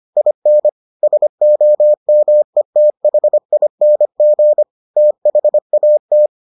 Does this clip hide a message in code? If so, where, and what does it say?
Yes: Morse code from 0.2 to 6.3 s, heard as a keyed tone, about 25 wpm, 600 Hz. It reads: IN SOMETHING THAT